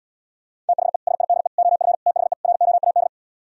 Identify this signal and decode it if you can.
Morse code, heard as a keyed tone, about 80 wpm, 690 Hz. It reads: THE SIDE OF THE ROAD